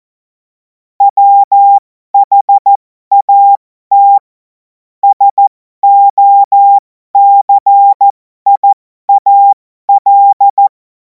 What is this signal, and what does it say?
Morse code, heard as a keyed tone, about 14 wpm, spaced as for 12 wpm, 800 Hz. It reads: WHAT SOCIAL
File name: Morsekurs / Witze